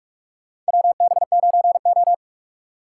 Morse code, heard as a keyed tone, about 44 wpm, 700 Hz. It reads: WB9O
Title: USA Call Signs - 4 Characters - 2x Character Spacing - 45wpm